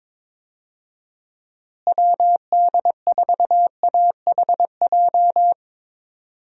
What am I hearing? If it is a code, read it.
Morse code, heard as a keyed tone, about 22 wpm, 700 Hz. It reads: WD4AHJ
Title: USA Call Signs - 6 Characters - Repeated 2x - 22wpm